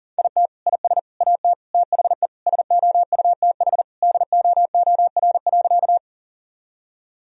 Morse code, heard as a keyed tone, about 40 wpm, 700 Hz. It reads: IT IS AT THE SOUTH DOOR.
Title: Sentences from Top 1000 Words - Review Part 2 - 40wpm